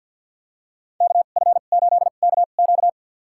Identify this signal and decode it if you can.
Morse code, heard as a keyed tone, about 49 wpm, spaced as for 35 wpm, 700 Hz. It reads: KF8KQ
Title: USA Call Signs - 5 Characters - 2x Character Spacing - 50wpm